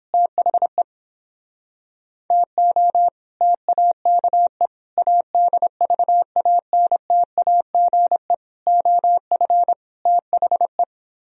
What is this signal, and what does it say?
Morse code, heard as a keyed tone, about 29 wpm, 700 Hz. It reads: THE TO TAKE ADVANTAGE OF THE